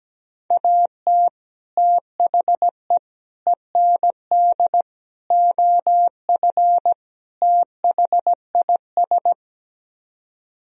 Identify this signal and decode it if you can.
Morse code, heard as a keyed tone, about 17 wpm, 700 Hz. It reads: AT THE END OF THIS
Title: Speed-Racing - Sets of 5 Words 17wpm